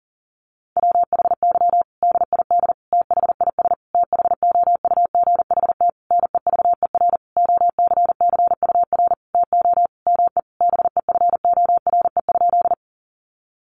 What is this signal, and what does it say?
Morse code, heard as a keyed tone, about 40 wpm, 700 Hz. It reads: WHY DID THIS THOUGHT NEVER OCCUR TO ME BEFORE?
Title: Sentences from Top 900 Words 40wpm